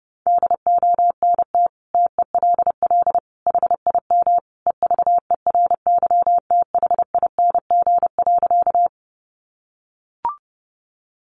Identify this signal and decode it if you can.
Morse code, heard as a keyed tone, about 30 wpm, 700 Hz. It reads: DONT TELL HIM EVERYTHING.
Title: Sentences from Top 700 Words 30wpm